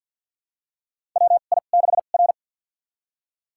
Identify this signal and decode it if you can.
Morse code, heard as a keyed tone, about 48 wpm, spaced as for 35 wpm, 700 Hz. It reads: WI6R
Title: USA Call Signs - 4 Characters - 2x Character Spacing - 50wpm